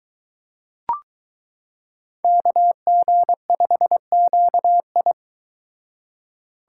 Morse code, heard as a keyed tone, about 23 wpm, 700 Hz. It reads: KG5QI